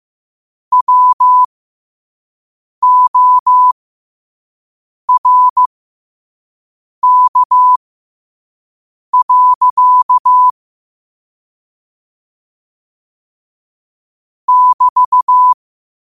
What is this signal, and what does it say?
Morse code, heard as a keyed tone, about 15 wpm, spaced as for 5 wpm, 1000 Hz. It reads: WORK. <BT>
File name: qotdfiles / 5wpm